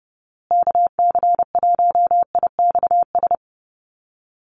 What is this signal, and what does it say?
Morse code, heard as a keyed tone, about 30 wpm, 700 Hz. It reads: KC1IXS
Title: USA Call Signs Repeated 3x 30wpm